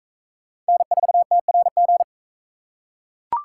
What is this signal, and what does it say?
Morse code, heard as a keyed tone, about 44 wpm, 700 Hz. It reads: N4TRG